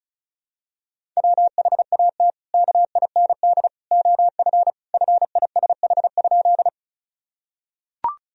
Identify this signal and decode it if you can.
Morse code, heard as a keyed tone, about 35 wpm, 700 Hz. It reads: WHAT KIND OF FISH?
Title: Sentences from Top 1000 Words - Review Part 1 - 35wpm